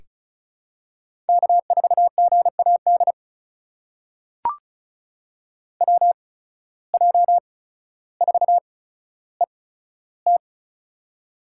Morse code, heard as a keyed tone, about 36 wpm, spaced as for 15 wpm, 700 Hz. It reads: K4GAD WJ4ET
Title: USA Call Signs - 5 Characters - 8x Character Spacing - 35wpm